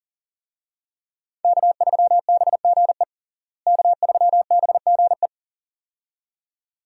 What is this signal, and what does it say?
Morse code, heard as a keyed tone, about 40 wpm, 700 Hz. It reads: K3BGE K3BGE
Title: USA Call Signs - 5 Characters - Repeated 2x - 40wpm